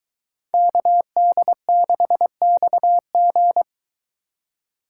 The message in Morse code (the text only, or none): KD6XG